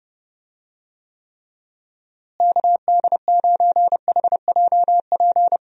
Morse code, heard as a keyed tone, about 30 wpm, 700 Hz. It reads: KD9HJP
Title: USA Call Signs - 6 Characters - Repeated 2x - 30wpm